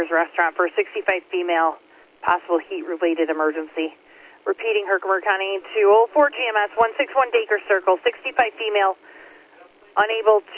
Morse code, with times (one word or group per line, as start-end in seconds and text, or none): none